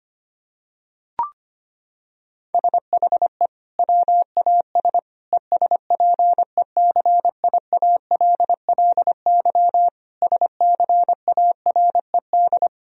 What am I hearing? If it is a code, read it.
Morse code, heard as a keyed tone, about 25 wpm, 700 Hz. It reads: SHE WAS ESPECIALLY SCARED
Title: Sentences from Top 1000 Words - Review Part 4 - 25wpm